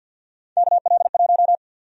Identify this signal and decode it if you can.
Morse code, heard as a keyed tone, about 51 wpm, 700 Hz. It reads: KL1
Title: USA Call Sign Prefixes - Repeated 2x - 50wpm